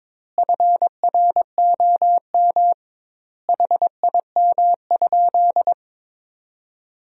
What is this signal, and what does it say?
Morse code, heard as a keyed tone, about 22 wpm, 700 Hz. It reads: FROM HIM?
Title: 2x Word-Spacing - Sentences from Top 300 Words - 22wpm